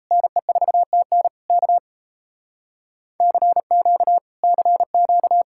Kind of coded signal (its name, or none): Morse code